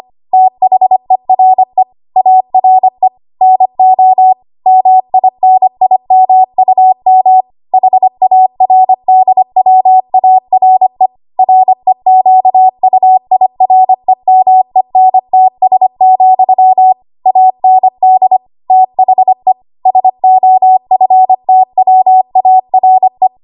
Morse code, heard as a keyed tone, about 25 wpm, 750 Hz. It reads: THERE ARE NO MINIMUM HARDWARE REQUIREMENTS, AND THE SOFTWARE
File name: Archive / 25 WPM / 25WPM